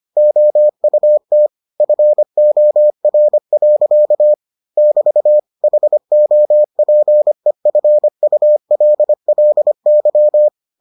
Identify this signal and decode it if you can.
Morse code, heard as a keyed tone, about 25 wpm, 600 Hz. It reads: OUT FOR. <BT> HOPEFULLY